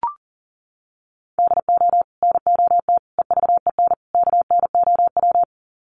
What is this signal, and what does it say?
Morse code, heard as a keyed tone, about 40 wpm, 700 Hz. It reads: DO NOT EVEN KNOW
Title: Sets of 4 Words 40wpm